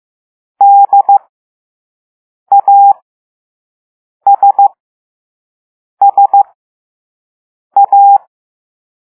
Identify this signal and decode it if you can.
Morse code, heard as a keyed tone, about 15 wpm, spaced as for 5 wpm, 800 Hz. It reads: DASSA